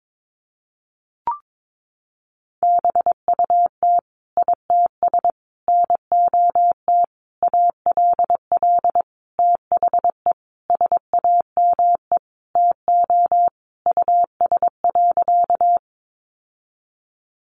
Morse code, heard as a keyed tone, about 22 wpm, 700 Hz. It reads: BUT ITS NOT ALL THE SAME TO US.
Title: Sentences from Top 600 Words 22wpm